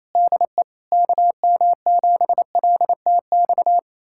Morse code, heard as a keyed tone, about 28 wpm, 700 Hz. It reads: DE KM7LTX
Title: USA Callsign DE Callsign 28wpm